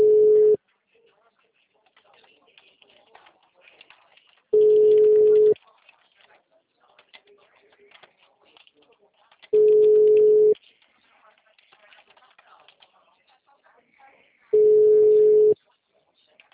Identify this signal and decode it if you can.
Ringback tone (Europe, 425 Hz), with speech.